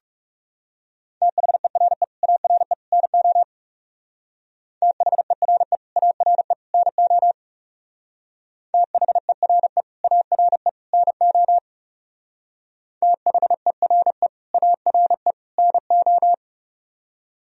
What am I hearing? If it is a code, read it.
Morse code, heard as a keyed tone, about 37 wpm, 700 Hz. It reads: THERE ARE NO THERE ARE NO THERE ARE NO THERE ARE NO
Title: Speed-Racing - Sets of 3 Words 30wpm